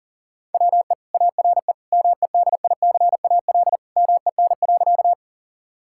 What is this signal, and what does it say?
Morse code, heard as a keyed tone, about 40 wpm, 700 Hz. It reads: WE ARE MEDICAL MEN.
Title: Sentences from Top 1000 Words - Review Part 5 - 40wpm